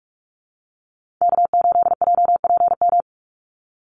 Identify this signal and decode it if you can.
Morse code, heard as a keyed tone, about 44 wpm, 700 Hz. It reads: K8JPM
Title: Call Signs - 45wpm